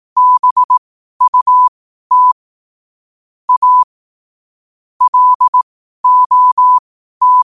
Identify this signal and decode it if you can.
Morse code, heard as a keyed tone, about 18 wpm, spaced as for 12 wpm, 1000 Hz. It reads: BUT A LOT
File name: qotdfiles / 12wpm